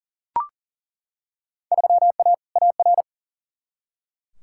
Morse code, heard as a keyed tone, about 40 wpm, 700 Hz. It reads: 3A AR
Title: Field Day Exchanges Encore - 40wpm